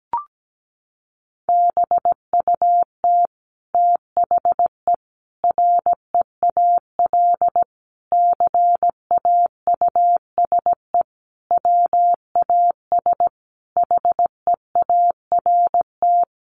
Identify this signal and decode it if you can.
Morse code, heard as a keyed tone, about 17 wpm, 700 Hz. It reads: BUT THE REAL CAUSE WAS HEART
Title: Sentences from Top 1200 Words 17wpm